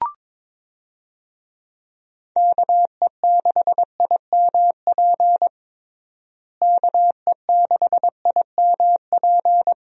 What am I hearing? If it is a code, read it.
Morse code, heard as a keyed tone, about 22 wpm, 700 Hz. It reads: KE6IMP KE6IMP
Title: USA Call Signs - 6 Characters - Repeated 3x - 22wpm